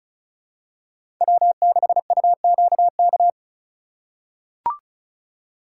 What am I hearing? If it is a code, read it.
Morse code, heard as a keyed tone, about 36 wpm, 700 Hz. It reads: W6UQK
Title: USA Call Signs - 5 Characters - 4x Character Spacing - 35wpm